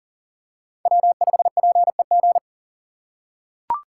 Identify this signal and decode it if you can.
Morse code, heard as a keyed tone, about 40 wpm, 700 Hz. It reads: W5PEG
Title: USA Call Signs - 3x Character Spacing - 40wpm